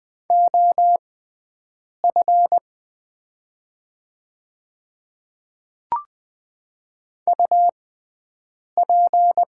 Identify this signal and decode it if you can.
Morse code, heard as a keyed tone, about 20 wpm, spaced as for 7 wpm, 700 Hz. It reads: OF UP
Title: 6x Character Spacing - Top 100 Words - 20wpm